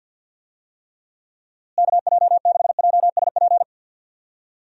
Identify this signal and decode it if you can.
Morse code, heard as a keyed tone, about 50 wpm, 700 Hz. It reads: KJ6JSP
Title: USA Call Signs - 6 Characters - Repeated 2x - 50wpm